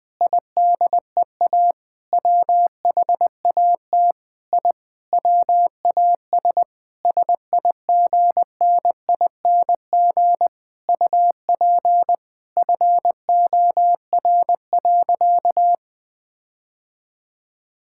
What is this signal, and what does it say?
Morse code, heard as a keyed tone, about 21 wpm, 700 Hz. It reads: IDEA WHAT I WAS SIGNING UP FOR.
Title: Sentences with 7-Word Phrases - 20wpm